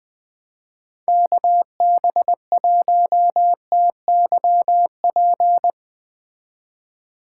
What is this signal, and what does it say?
Morse code, heard as a keyed tone, about 20 wpm, 700 Hz. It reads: KB1TYP
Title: Speed Racing - USA 2x3 Call Signs 17wpm